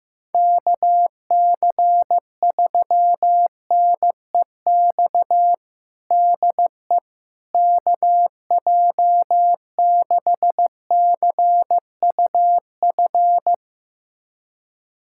Morse code, heard as a keyed tone, about 15 wpm, 700 Hz. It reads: KC3NEX DE KJ6CUF
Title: USA Callsign DE Callsign - 6 Characters - 15wpm